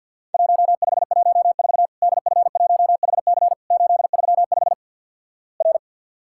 Morse code, heard as a keyed tone, about 50 wpm, 700 Hz. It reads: 1514 DF1HC 835 R
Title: Worked All Europe DX Contest - QTC Exchange - 50wpm